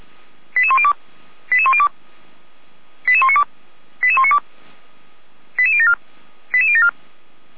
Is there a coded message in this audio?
Yes: five-tone selective call (ZVEI) from 0.6 to 6.9 s, heard as quick runs of five tones, tones of about 70 ms. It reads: 88102 88102 88192 88192 88074 88074